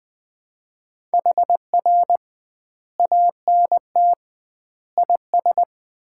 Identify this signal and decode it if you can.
Morse code, heard as a keyed tone, about 21 wpm, 700 Hz. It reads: HR ANT IS
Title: Common QSO Phrases - 2x Word-Spacing - 20wpm